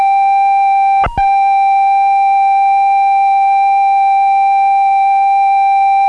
Computer data on cassette.